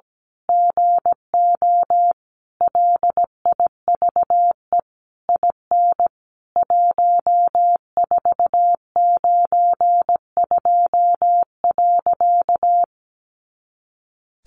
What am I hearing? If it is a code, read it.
Morse code, heard as a keyed tone, about 17 wpm, 700 Hz. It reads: GO LIVE IN 1492.
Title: A Sound Of Thunder 17wpm